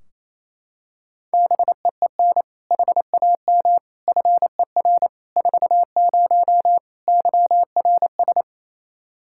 Morse code, heard as a keyed tone, about 27 wpm, 700 Hz. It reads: BEEN HAM FER 40 YRS
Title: Practice QSO - Exchange 3 - More Info - 28wpm